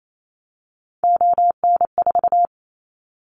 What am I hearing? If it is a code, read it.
Morse code, heard as a keyed tone, about 28 wpm, 700 Hz. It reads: ON4
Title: Callsign Prefixes for Top 50 DXCC - 28wpm